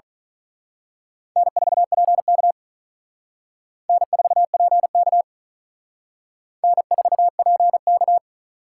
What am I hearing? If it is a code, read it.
Morse code, heard as a keyed tone, about 39 wpm, 700 Hz. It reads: N4PK N4PK N4PK
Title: Speed Racing - USA 1x2 Call Signs 35wpm